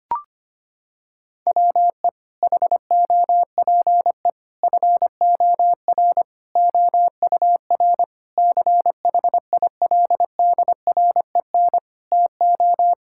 Morse code, heard as a keyed tone, about 25 wpm, 700 Hz. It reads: WE HOPE FOR OUR CHILDREN TO
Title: Sentences from Top 1000 Words - Review Part 6 - 25wpm